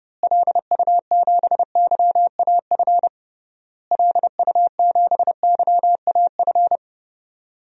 Morse code, heard as a keyed tone, about 30 wpm, 700 Hz. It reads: LU7YAF LU7YAF
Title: DXCC-100 Call Signs - 6 Characters - Repeated 3x - 30wpm